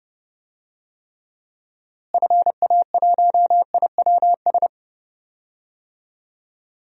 Morse code, heard as a keyed tone, about 30 wpm, 700 Hz. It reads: FA1IWS
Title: DXCC-100 Call Signs - 6 Characters - Repeated 3x - 30wpm